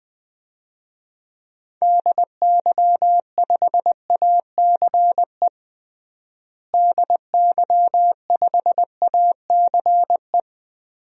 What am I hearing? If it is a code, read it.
Morse code, heard as a keyed tone, about 20 wpm, 700 Hz. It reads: DY5ACE DY5ACE
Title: DXCC-100 Call Signs - 6 Characters - Repeated 2x - 20wpm